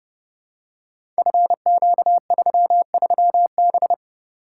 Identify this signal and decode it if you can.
Morse code, heard as a keyed tone, about 30 wpm, 700 Hz. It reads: FQ33B
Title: Code Groups - 5 Letter:Numbers - 30wpm